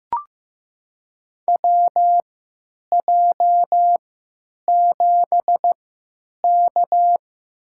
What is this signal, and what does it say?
Morse code, heard as a keyed tone, about 15 wpm, spaced as for 9 wpm, 700 Hz. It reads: WJ7K
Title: USA Call Signs - 4 Characters - 3x Character Spacing - 15wpm